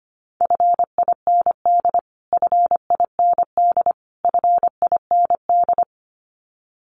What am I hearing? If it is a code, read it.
Morse code, heard as a keyed tone, about 25 wpm, 700 Hz. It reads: FIND FIND FIND
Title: Top 500 Words Repeated 25wpm